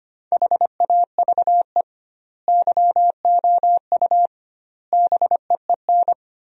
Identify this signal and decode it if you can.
Morse code, heard as a keyed tone, about 25 wpm, 700 Hz. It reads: HAVE YOU BEEN
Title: 2x Word-Spacing - Sentences from Top 200 Words - 25wpm